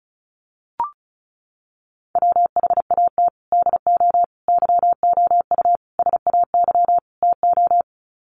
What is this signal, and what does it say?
Morse code, heard as a keyed tone, about 34 wpm, 700 Hz. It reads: WHAT DO YOU SAY TO
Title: Sets of 5 Words 35wpm